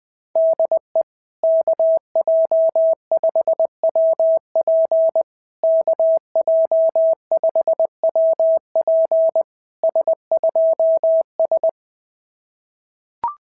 Morse code, heard as a keyed tone, about 19 wpm, 650 Hz. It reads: DE KJ5WP KJ5WP S2S
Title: SOTA QSO Practice 20wpm